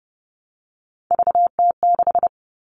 Morse code, heard as a keyed tone, about 30 wpm, 700 Hz. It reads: VT6
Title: All Cut Numbers - 3-Digit 30wpm